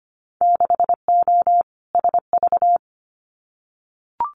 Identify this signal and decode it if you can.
Morse code, heard as a keyed tone, about 25 wpm, 690 Hz. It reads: 6O SV
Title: Winter Field Day Exchanges Encore 25wpm